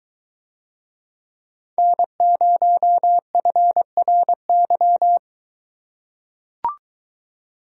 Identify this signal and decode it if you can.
Morse code, heard as a keyed tone, about 23 wpm, 700 Hz. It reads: N0FRY